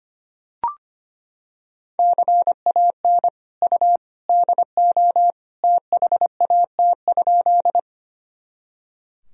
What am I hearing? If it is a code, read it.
Morse code, heard as a keyed tone, about 25 wpm, 700 Hz. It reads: CAN U DO THAT?
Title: Sentences from Top 100 Words and Abbreviations 25wpm